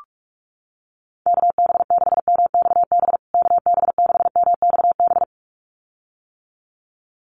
Morse code, heard as a keyed tone, about 44 wpm, 700 Hz. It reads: KB6MXB KB6MXB
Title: USA Call Signs Repeated 2x 45wpm